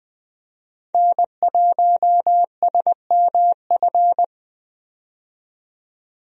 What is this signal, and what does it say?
Morse code, heard as a keyed tone, about 20 wpm, 700 Hz. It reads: N1SMF
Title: USA Call Signs - 5 Characters - 20wpm